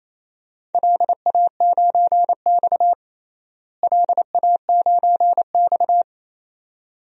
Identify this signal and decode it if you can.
Morse code, heard as a keyed tone, about 28 wpm, 700 Hz. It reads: LA9X LA9X
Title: DXCC-100 Call Signs - 4 Characters - Repeated 2x - 28wpm